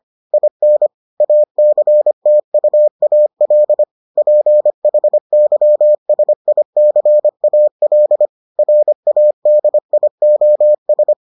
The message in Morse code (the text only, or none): IN ACTUAL PHYSICAL RADIOS